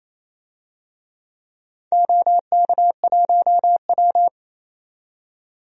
Morse code, heard as a keyed tone, about 28 wpm, 700 Hz. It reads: OK1W
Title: DXCC-100 Call Signs - 4 Characters - 28wpm